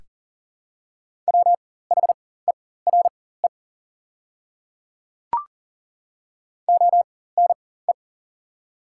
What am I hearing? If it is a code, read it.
Morse code, heard as a keyed tone, about 40 wpm, spaced as for 19 wpm, 700 Hz. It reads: WHERE ONE
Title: 4x Character Spacing - Top 200 Words - 40wpm